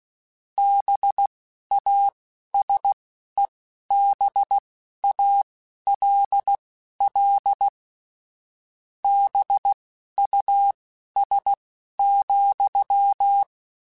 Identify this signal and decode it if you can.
Morse code, heard as a keyed tone, about 16 wpm, spaced as for 12 wpm, 800 Hz. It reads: BASEBALL BUS,
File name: Misc / Misc-mp3s / BIG @10